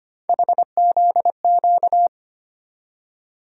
Morse code, heard as a keyed tone, about 25 wpm, 700 Hz. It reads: HZQ
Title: USA Call Sign Postfixes - 3 Letters 25wpm